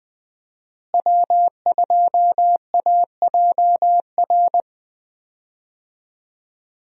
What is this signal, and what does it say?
Morse code, heard as a keyed tone, about 20 wpm, 700 Hz. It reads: W2AJR
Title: USA Call Signs - 5 Characters - Repeated 3x - 20wpm